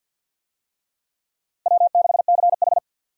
Morse code, heard as a keyed tone, about 47 wpm, 700 Hz. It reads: W6CH
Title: USA Call Signs - 4 Characters - Repeated 3x - 50wpm